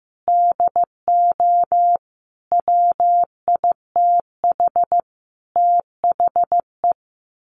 Morse code, heard as a keyed tone, about 15 wpm, 700 Hz. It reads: DO WITH THE
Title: Sets of 5 Words 15wpm